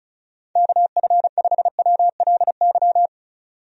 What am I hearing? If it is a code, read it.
Morse code, heard as a keyed tone, about 36 wpm, 700 Hz. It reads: KF5WLY
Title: USA Call Signs - 6 Characters - 35wpm